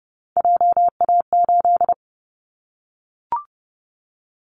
Morse code, heard as a keyed tone, about 30 wpm, 700 Hz. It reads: JA8
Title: Callsign Prefixes for Top 25 DXCC - Repeated 3x - 30wpm